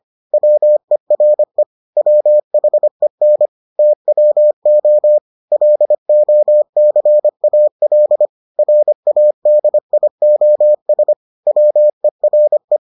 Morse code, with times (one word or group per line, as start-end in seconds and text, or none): 0.3-1.6 WERE
2.0-3.5 WHEN
3.8-5.2 TWO
5.5-8.3 LOCAL
8.6-11.1 RADIOS
11.5-12.8 WERE